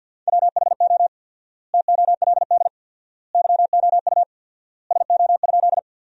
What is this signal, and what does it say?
Morse code, heard as a keyed tone, about 50 wpm, 700 Hz. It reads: WHO TOLD YOU SO?
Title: 4x Word-Spacing - Sentences from Top 400 Words - 50wpm